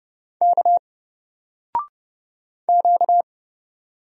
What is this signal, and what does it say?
Morse code, heard as a keyed tone, about 31 wpm, 700 Hz. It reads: K Q
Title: Single Letter-Number 30wpm